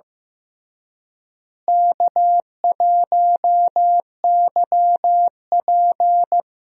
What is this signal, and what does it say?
Morse code, heard as a keyed tone, about 16 wpm, 700 Hz. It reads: K1YP
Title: USA Call Signs - 4 Characters - Repeated 3x - 15wpm